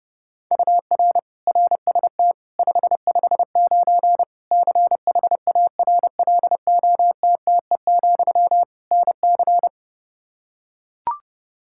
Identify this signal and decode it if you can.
Morse code, heard as a keyed tone, about 30 wpm, 700 Hz. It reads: UR RST 559 CHARLOTTE, NC
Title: Practice QSO - Exchange 1 - Beginning - 30wpm